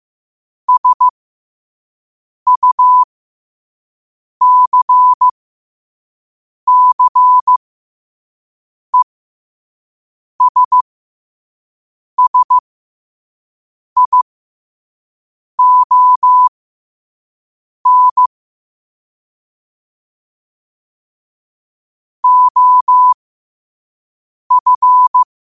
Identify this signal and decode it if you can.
Morse code, heard as a keyed tone, about 15 wpm, spaced as for 5 wpm, 1000 Hz. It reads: SUCCESSION OF
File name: qotdfiles / 5wpm